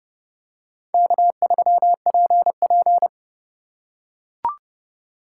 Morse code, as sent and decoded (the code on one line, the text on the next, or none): -.- ...-- .--. .--.
K3PP